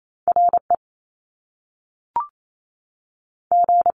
Morse code, heard as a keyed tone, about 28 wpm, 700 Hz. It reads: R E G
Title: Top Two Letter Combinations in Words 28wpm